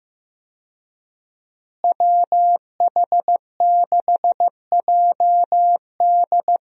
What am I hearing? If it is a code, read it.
Morse code, heard as a keyed tone, about 16 wpm, 700 Hz. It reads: WH6JD
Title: USA Call Signs - 5 Characters - Repeated 3x - 15wpm